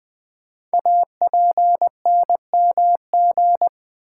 Morse code, heard as a keyed tone, about 21 wpm, 700 Hz. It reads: APNMG
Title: Code Groups - 5 Letters - 20wpm